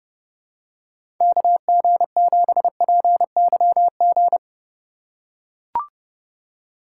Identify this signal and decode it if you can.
Morse code, heard as a keyed tone, about 32 wpm, 700 Hz. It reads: KG7PYG
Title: USA Call Signs - 2x Character Spacing - 30wpm